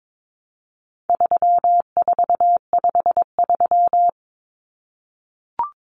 Morse code, heard as a keyed tone, about 22 wpm, 700 Hz. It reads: 3453
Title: Numbers - 4-Digit 22wpm